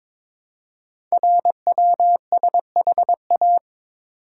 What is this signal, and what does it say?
Morse code, heard as a keyed tone, about 22 wpm, 700 Hz. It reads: RWSHA
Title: Code Groups - 5 Letters - 22wpm